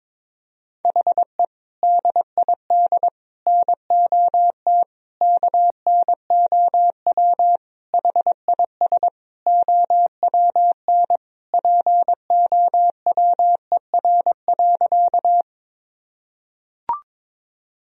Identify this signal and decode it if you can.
Morse code, heard as a keyed tone, about 22 wpm, 700 Hz. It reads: HE DID NOT KNOW HIS OWN POWER.
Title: Sentences from Top 1000 Words - Review Part 2 - 22wpm